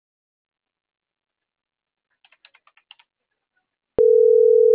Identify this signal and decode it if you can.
Ringback tone (North America), 440 + 480 Hz.